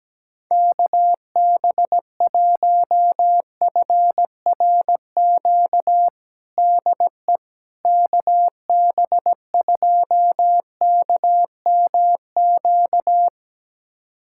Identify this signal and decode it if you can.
Morse code, heard as a keyed tone, about 17 wpm, 700 Hz. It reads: KB1FRQ DE KB2KMQ
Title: USA Callsign DE Callsign - 6 Characters - 17wpm